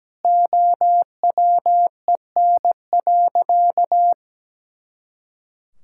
Morse code, heard as a keyed tone, about 17 wpm, 700 Hz. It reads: OWEN.
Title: Pangram Challenge v2 - 17wpm